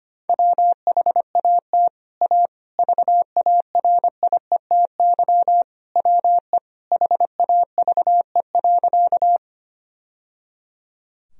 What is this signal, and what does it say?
Morse code, heard as a keyed tone, about 25 wpm, 700 Hz. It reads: WHAT A VARIETY WE HAVE.
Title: Sentences from Top 1000 Words - Review Part 4 - 25wpm